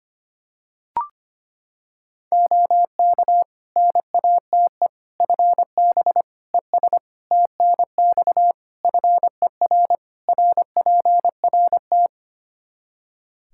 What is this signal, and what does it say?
Morse code, heard as a keyed tone, about 25 wpm, 700 Hz. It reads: OK NATE FB ES TNX FER RPRT
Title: Practice QSO - Exchange 2 - Info - 25wpm